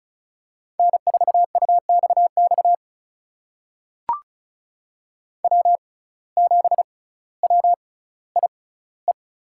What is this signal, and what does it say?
Morse code, heard as a keyed tone, about 36 wpm, spaced as for 18 wpm, 700 Hz. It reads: N4UXX W7WIE
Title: USA Call Signs - 5 Characters - 6x Character Spacing - 35wpm